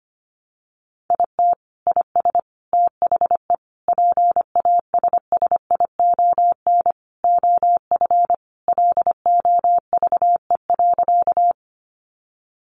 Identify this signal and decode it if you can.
Morse code, heard as a keyed tone, about 25 wpm, 700 Hz. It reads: IT IS THE PASSION OF LOVE.
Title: Sentences from Top 1300 Words 25wpm